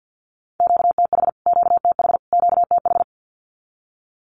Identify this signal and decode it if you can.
Morse code, heard as a keyed tone, about 49 wpm, 700 Hz. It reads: QTH QTH QTH
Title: Q-Codes Repeated 3x 50wpm